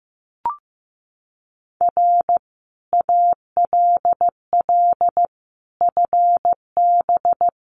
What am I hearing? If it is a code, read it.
Morse code, heard as a keyed tone, about 15 wpm, 700 Hz. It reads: R ALL FB
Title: Common QSO Phrases 15wpm